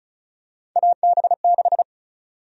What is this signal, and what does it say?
Morse code, heard as a keyed tone, about 36 wpm, 700 Hz. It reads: AB6
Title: USA Call Sign Prefixes - 35wpm